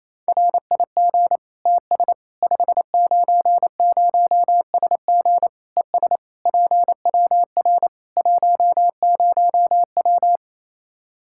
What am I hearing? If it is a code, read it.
Morse code, heard as a keyed tone, about 27 wpm, 700 Hz. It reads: RIG TS 590SG ES PWR 10W
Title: Practice QSO - Exchange 2 - Info - 28wpm